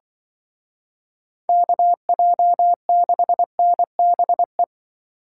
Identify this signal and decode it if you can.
Morse code, heard as a keyed tone, about 24 wpm, 700 Hz. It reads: KJ6NBE
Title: Speed Racing - USA 2x3 Call Signs 20wpm